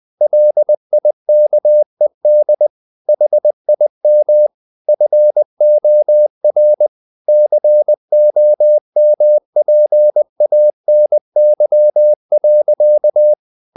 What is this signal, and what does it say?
Morse code, heard as a keyed tone, about 20 wpm, 600 Hz. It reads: LIKED HIM FOR COMPANY.
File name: the old man and the sea / the old man and the sea part 1 20wpm